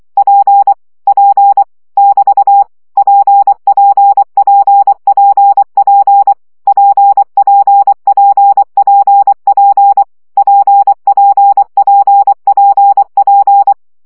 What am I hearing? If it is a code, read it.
Morse code, heard as a keyed tone, about 24 wpm, 800 Hz. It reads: P P <BT> PPPPP PPPPP PPPPP
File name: CW gyakorlas p 20 baud